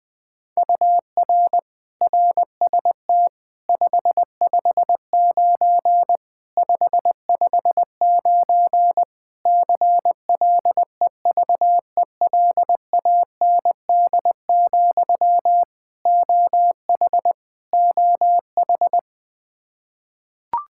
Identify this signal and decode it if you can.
Morse code, heard as a keyed tone, about 20 wpm, 700 Hz. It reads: UR RST 559 559 CLEVELAND, OH OH
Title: Practice QSO - Exchange 1 - Beginning - 20wpm